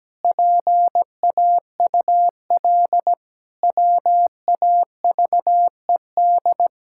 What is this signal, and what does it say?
Morse code, heard as a keyed tone, about 17 wpm, 700 Hz. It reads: PAUL WAVED
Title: Pangram Challenge v2 - 17wpm